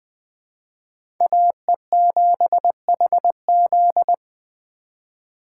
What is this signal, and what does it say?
Morse code, heard as a keyed tone, about 20 wpm, 700 Hz. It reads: AE7HZ
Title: USA Call Signs - 5 Characters - 2x Character Spacing - 20wpm